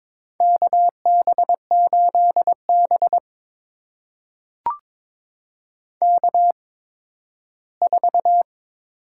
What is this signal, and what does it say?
Morse code, heard as a keyed tone, about 23 wpm, spaced as for 13 wpm, 700 Hz. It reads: KB8B K4
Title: USA Call Signs - 8x Character Spacing - 22wpm